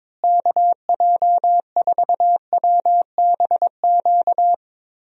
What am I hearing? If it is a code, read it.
Morse code, heard as a keyed tone, about 22 wpm, 700 Hz. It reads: KJ4WBQ
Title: USA Call Signs - 6 Characters - Repeated 2x - 22wpm